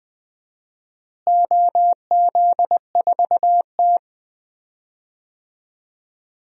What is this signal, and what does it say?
Morse code, heard as a keyed tone, about 21 wpm, 700 Hz. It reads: OZ4T
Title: DXCC-100 Call Signs - 4 Characters - Repeated 2x - 20wpm